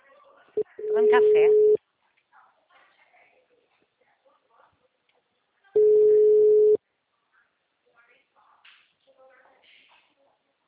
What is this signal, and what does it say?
Ringback tone (Europe, 425 Hz), with speech.